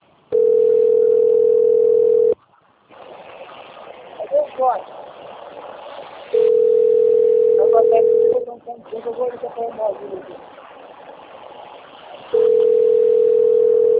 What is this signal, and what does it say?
Ringback tone (North America), 440 + 480 Hz, with speech.
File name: rg-1600-unknown- - - .31466